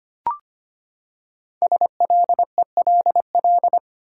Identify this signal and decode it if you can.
Morse code, heard as a keyed tone, about 25 wpm, 700 Hz. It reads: SLELL
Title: Code Groups - 5 Letters - 25wpm